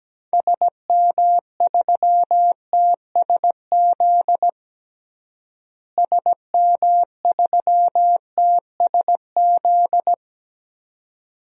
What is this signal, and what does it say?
Morse code, heard as a keyed tone, about 17 wpm, 700 Hz. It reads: SM3TSZ SM3TSZ
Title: DXCC-100 Call Signs - 6 Characters - Repeated 2x - 17wpm